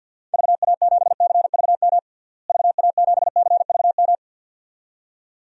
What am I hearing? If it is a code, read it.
Morse code, heard as a keyed tone, about 49 wpm, 700 Hz. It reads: VA7CVM VA7CVM
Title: DXCC-100 Call Signs - 6 Characters - Repeated 2x - 50wpm